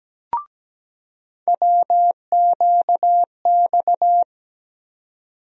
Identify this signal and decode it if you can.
Morse code, heard as a keyed tone, about 17 wpm, 700 Hz. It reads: WQX
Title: USA Call Sign Postfixes - 3 Letters 17wpm